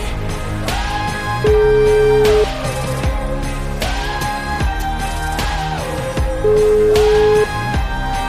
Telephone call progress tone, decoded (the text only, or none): ringback tone